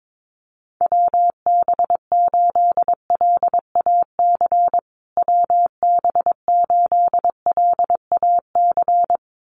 Morse code, heard as a keyed tone, about 22 wpm, 700 Hz. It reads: WB8LAC WB8LAC
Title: USA Call Signs Repeated 2x 22wpm